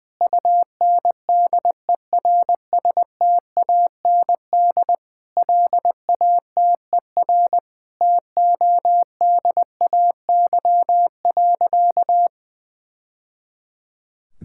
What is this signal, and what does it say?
Morse code, heard as a keyed tone, about 20 wpm, 700 Hz. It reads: UNDERSTAND LATER TODAY.
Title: Sentences from Top 1000 Words - Review Part 5 - 20wpm